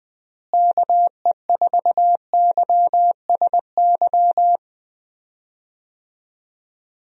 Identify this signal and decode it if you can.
Morse code, heard as a keyed tone, about 21 wpm, 700 Hz. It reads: KE4YSY
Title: USA Call Signs - 6 Characters - Repeated 2x - 20wpm